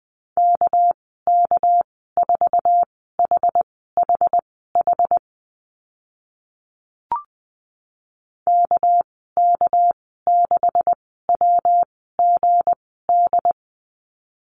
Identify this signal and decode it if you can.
Morse code, heard as a keyed tone, about 20 wpm, spaced as for 14 wpm, 700 Hz. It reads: KK4HHH KK6WGD
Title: 200 Lesson 32g - Letter k - Call Signs 20wpm